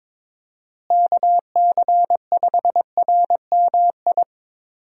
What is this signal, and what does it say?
Morse code, heard as a keyed tone, about 21 wpm, 700 Hz. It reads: KC5RMI
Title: USA Call Signs - 6 Characters - Repeated 3x - 22wpm